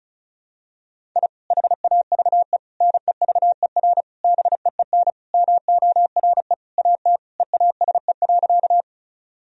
Morse code, heard as a keyed tone, about 34 wpm, 700 Hz. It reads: I HAVE NEVER BEEN MORE AT EASE.
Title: Sentences from Top 1000 Words - Review Part 1 - 35wpm